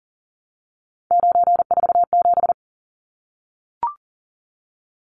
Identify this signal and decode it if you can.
Morse code, heard as a keyed tone, about 40 wpm, 700 Hz. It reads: 947
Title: Numbers - 3-Digit 40wpm